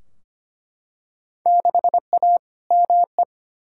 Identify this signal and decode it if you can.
Morse code, heard as a keyed tone, about 25 wpm, 700 Hz. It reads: 6A ME
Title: Field Day Exchanges - 25wpm